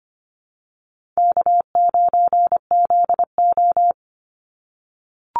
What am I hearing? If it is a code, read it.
Morse code, heard as a keyed tone, about 25 wpm, 700 Hz. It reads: K9ZO
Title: Call Signs - 25wpm